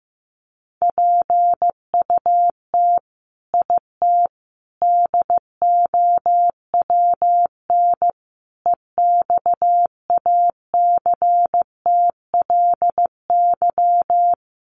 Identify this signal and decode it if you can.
Morse code, heard as a keyed tone, about 15 wpm, 700 Hz. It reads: PUT IT DOWN EXACTLY
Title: Sentences from Top 900 Words 15wpm